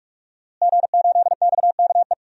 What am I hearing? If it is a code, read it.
Morse code, heard as a keyed tone, about 41 wpm, 700 Hz. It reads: G8XKE
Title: DXCC-100 Call Signs - 5 Characters - Repeated 2x - 45wpm